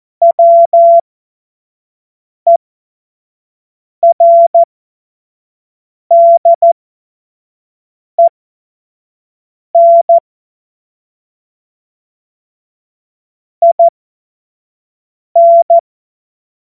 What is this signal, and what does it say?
Morse code, heard as a keyed tone, about 14 wpm, spaced as for 5 wpm, 680 Hz. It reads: WERDEN IN
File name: Morsekurs / Witze / Witz 16. Mai 2023 25 F70